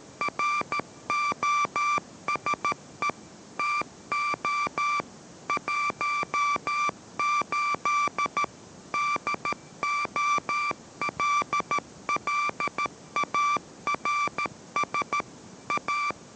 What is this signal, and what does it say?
Morse code, heard as a keyed tone, about 13 wpm, 1170 Hz, with noise. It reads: ROSE TO 18 DOLLARS A